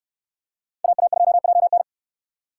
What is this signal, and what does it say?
Morse code, heard as a keyed tone, about 69 wpm, 700 Hz. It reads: AA2JN